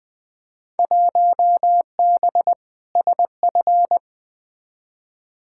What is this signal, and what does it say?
Morse code, heard as a keyed tone, about 20 wpm, 700 Hz. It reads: 1B SF
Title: Field Day Exchanges Encore - 20wpm